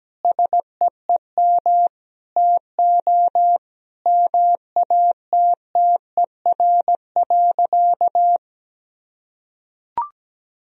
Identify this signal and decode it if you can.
Morse code, heard as a keyed tone, about 17 wpm, 700 Hz. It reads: SEEM TO MATTER.
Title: Sentences from Top 1000 Words - Review Part 1 - 17wpm